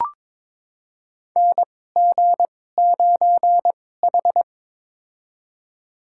Morse code, heard as a keyed tone, about 22 wpm, spaced as for 16 wpm, 700 Hz. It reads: NG9H
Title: USA Call Signs - 4 Characters - 2x Character Spacing - 22wpm